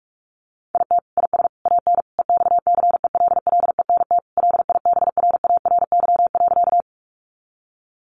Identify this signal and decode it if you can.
Morse code, heard as a keyed tone, about 44 wpm, 700 Hz. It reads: IT IS AN EXCELLENT LIBRARY.
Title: Sentences from Top 1000 Words 45wpm